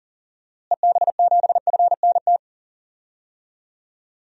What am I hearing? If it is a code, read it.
Morse code, heard as a keyed tone, about 40 wpm, 700 Hz. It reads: EB7FNT
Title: DXCC-100 Call Signs - 6 Characters - Repeated 2x - 40wpm